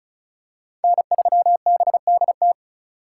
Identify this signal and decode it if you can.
Morse code, heard as a keyed tone, about 34 wpm, 700 Hz. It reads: N3BDT
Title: USA Call Signs - 5 Characters - 4x Character Spacing - 35wpm